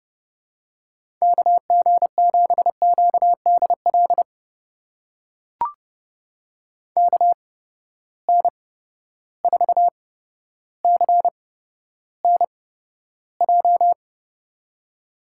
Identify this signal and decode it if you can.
Morse code, heard as a keyed tone, about 30 wpm, spaced as for 13 wpm, 700 Hz. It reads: KG7QDL KN4CNJ
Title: USA Call Signs - 6 Characters - 8x Character Spacing - 30wpm